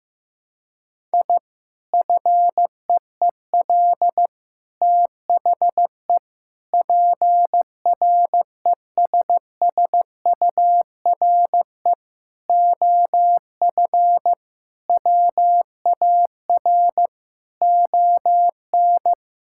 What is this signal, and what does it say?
Morse code, heard as a keyed tone, about 15 wpm, 700 Hz. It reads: I FEEL THE PRESSURE OF WAR ON
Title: Sentences from Top 1000 Words - Review Part 6 - 15wpm